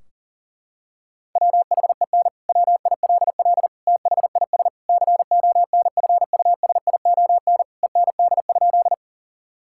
Morse code, heard as a keyed tone, about 40 wpm, 700 Hz. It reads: WHEN WILL THIS CONFUSION END?
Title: Sentences from Top 1600 Words 40wpm